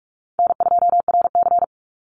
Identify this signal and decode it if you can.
Morse code, heard as a keyed tone, about 44 wpm, 700 Hz. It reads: N2RC
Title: Easy Call Signs Repeated 2x - 45wpm